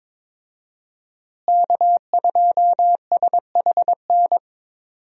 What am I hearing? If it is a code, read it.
Morse code, heard as a keyed tone, about 22 wpm, 700 Hz. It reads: K2SHN
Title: USA Call Signs - 5 Characters - Repeated 2x - 22wpm